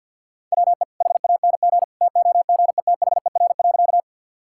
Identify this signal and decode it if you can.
Morse code, heard as a keyed tone, about 52 wpm, 700 Hz. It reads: WE HANG TOGETHER.
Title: Sentences from Top 1600 Words 50wpm